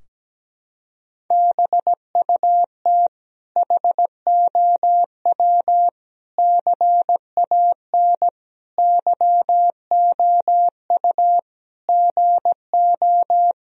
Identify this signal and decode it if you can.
Morse code, heard as a keyed tone, about 17 wpm, 700 Hz. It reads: BUT HOW CAN YOU GO
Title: Sentences from Top 1000 Words - Review Part 2 - 17wpm